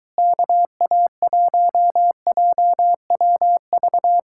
Morse code, heard as a keyed tone, about 23 wpm, 700 Hz. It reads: KA1JWV